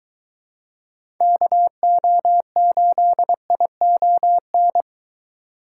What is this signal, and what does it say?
Morse code, heard as a keyed tone, about 22 wpm, 700 Hz. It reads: KO8ION